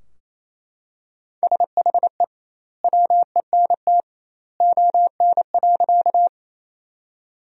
Morse code, heard as a keyed tone, about 28 wpm, 700 Hz. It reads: SHE WENT ON.
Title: 2x Word-Spacing - Sentences from Top 200 Words - 28wpm